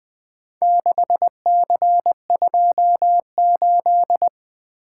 Morse code, heard as a keyed tone, about 20 wpm, 700 Hz. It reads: 6C28
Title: Four Letter-Number Combinations - 20wpm